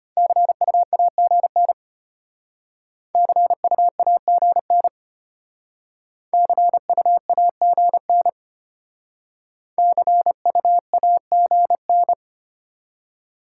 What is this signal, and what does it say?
Morse code, heard as a keyed tone, about 34 wpm, 700 Hz. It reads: CU AGN CU AGN CU AGN CU AGN